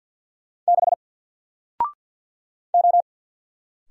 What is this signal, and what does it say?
Morse code, heard as a keyed tone, about 50 wpm, 700 Hz. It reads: 6 O
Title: Single Character (Letter:Number:Punctuation:Prosign) 50wpm